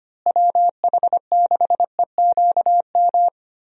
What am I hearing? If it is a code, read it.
Morse code, heard as a keyed tone, about 25 wpm, 700 Hz. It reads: WH6EQM
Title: USA Call Signs - 6 Characters - Repeated 2x - 25wpm